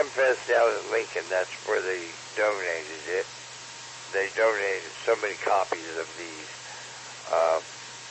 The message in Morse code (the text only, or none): none